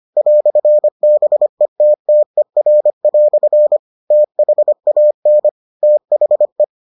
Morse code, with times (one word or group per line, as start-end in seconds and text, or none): 0.2-3.8 "BETTER"
4.1-5.5 THAN
5.8-6.6 THE